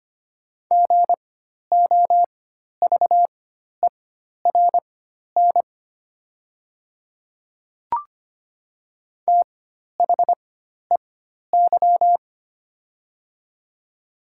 Morse code, heard as a keyed tone, about 25 wpm, spaced as for 12 wpm, 700 Hz. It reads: GOVERN THEY
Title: 4x Character Spacing - Top 500 Words - 25wpm